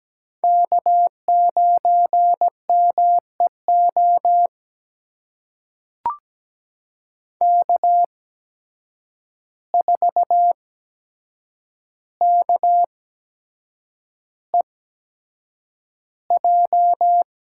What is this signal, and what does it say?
Morse code, heard as a keyed tone, about 17 wpm, spaced as for 7 wpm, 700 Hz. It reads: K9MEO K4KEJ
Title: USA Call Signs - 5 Characters - 8x Character Spacing - 17wpm